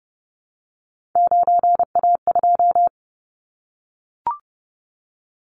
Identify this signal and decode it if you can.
Morse code, heard as a keyed tone, about 30 wpm, 700 Hz. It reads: 9A2
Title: Callsign Prefixes for Top 25 DXCC - 30wpm